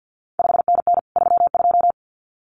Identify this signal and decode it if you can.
Morse code, heard as a keyed tone, about 50 wpm, 700 Hz. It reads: 5NN 32
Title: CQ WW DX Contest Exchange - 50wpm